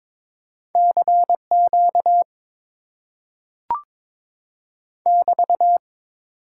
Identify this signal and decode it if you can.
Morse code, heard as a keyed tone, about 22 wpm, 700 Hz. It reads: CQ <BT>
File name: Top 90 QSO Elements - 8x Character Spacing - 22wpm